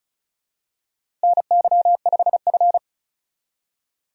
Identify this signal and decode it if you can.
Morse code, heard as a keyed tone, about 36 wpm, 700 Hz. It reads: NY5F
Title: USA Call Signs - 4 Characters - Repeated 3x - 35wpm